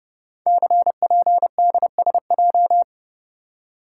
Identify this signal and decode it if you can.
Morse code, heard as a keyed tone, about 28 wpm, 700 Hz. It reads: CPDSJ
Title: Code Groups - 5 Letter:Numbers - 30wpm